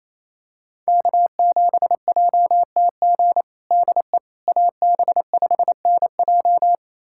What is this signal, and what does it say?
Morse code, heard as a keyed tone, about 28 wpm, 700 Hz. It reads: K7JTG DE AB5NJ
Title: USA Callsign DE Callsign - 5 Characters - 28wpm